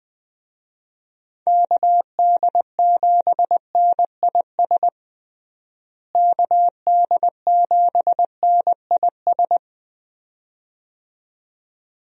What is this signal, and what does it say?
Morse code, heard as a keyed tone, about 20 wpm, 700 Hz. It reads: KD7NIS KD7NIS
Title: USA Call Signs - 6 Characters - Repeated 2x - 20wpm